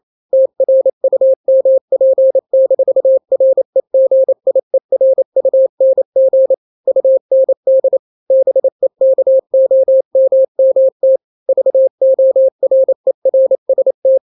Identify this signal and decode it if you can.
Morse code, heard as a keyed tone, about 28 wpm, 530 Hz. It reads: TRUMP-REGIERUNG UND BEKOMMT VORERST
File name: daily / 27wpm